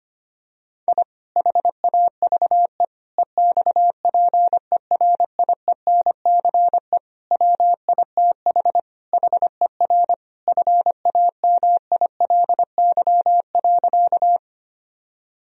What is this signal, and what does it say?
Morse code, heard as a keyed tone, about 25 wpm, 700 Hz. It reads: I HAVE EXPERIENCE WITH HER FAMILY.
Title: Sentences from Top 1000 Words - Review Part 3 - 25wpm